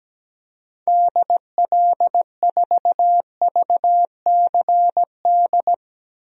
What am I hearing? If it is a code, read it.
Morse code, heard as a keyed tone, about 17 wpm, 700 Hz. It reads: DL4VCD
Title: DXCC-100 Call Signs - 6 Characters - Repeated 2x - 17wpm